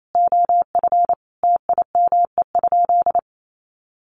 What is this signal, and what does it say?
Morse code, heard as a keyed tone, about 28 wpm, 700 Hz. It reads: OF TIME?
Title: Sentences from Top 1200 Words 28wpm